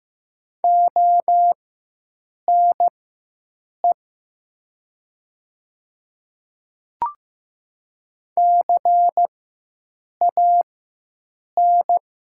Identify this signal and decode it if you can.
Morse code, heard as a keyed tone, about 15 wpm, spaced as for 7 wpm, 700 Hz. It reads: ONE CAN
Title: 4x Character Spacing - Top 100 Words - 15wpm